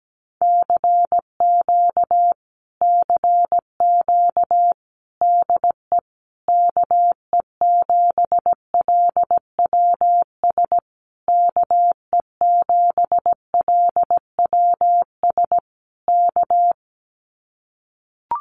Morse code, heard as a keyed tone, about 17 wpm, 700 Hz. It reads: CQ CQ DE KE7LWS KE7LWS K
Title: Common QSO Phrases Encore 17wpm